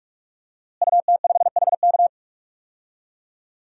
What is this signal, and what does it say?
Morse code, heard as a keyed tone, about 46 wpm, 700 Hz. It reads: UT5HK
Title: DXCC-100 Call Signs - 5 Characters - Repeated 2x - 45wpm